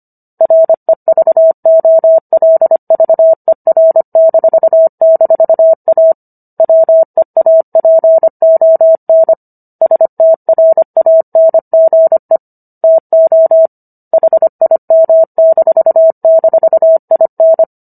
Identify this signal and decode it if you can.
Morse code, heard as a keyed tone, about 25 wpm, 650 Hz. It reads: REVOLVER--A WEAPON STRANGE TO HIM--IN